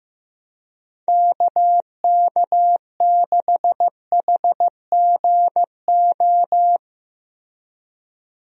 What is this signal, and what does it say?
Morse code, heard as a keyed tone, about 15 wpm, 700 Hz. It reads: KK6HGO
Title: USA Call Signs - 6 Characters - Repeated 2x - 15wpm